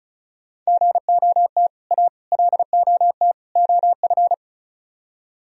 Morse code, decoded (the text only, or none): GOT A LOT OF